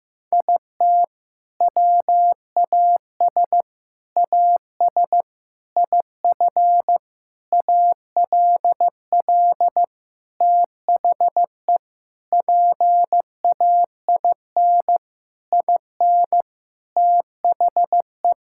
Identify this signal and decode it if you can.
Morse code, heard as a keyed tone, about 15 wpm, 700 Hz. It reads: IT WAS AS IF ALL THE PAIN IN THE
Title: Sentences from Top 1600 Words 15wpm